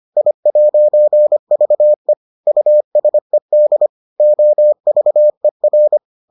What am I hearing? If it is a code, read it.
Morse code, heard as a keyed tone, about 25 wpm, 600 Hz. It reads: I'VE USED OVER